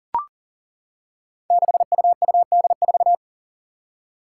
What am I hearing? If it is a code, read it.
Morse code, heard as a keyed tone, about 40 wpm, 700 Hz. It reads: 6UUD4
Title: Five Challenging Letter-Number Combinations 40wpm